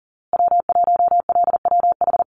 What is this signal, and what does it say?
Morse code, heard as a keyed tone, about 40 wpm, 700 Hz. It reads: W1LWH
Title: Call Signs Repeated 2x - 40wpm